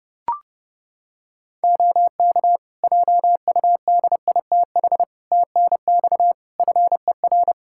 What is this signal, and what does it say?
Morse code, heard as a keyed tone, about 31 wpm, 700 Hz. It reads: OK JUDITH TNX FER
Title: Practice QSO - Exchange 4 - Ending - 30wpm